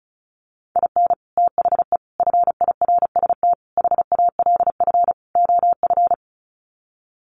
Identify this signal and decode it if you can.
Morse code, heard as a keyed tone, about 34 wpm, 700 Hz. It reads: IN THE FIRST HALF OF
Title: Sets of 5 Words 35wpm